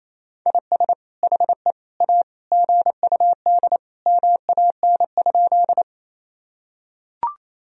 Morse code, heard as a keyed tone, about 28 wpm, 700 Hz. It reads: IS HE A GUD MAN?
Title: Sentences from Top 200 Words and Abbreviations 28wpm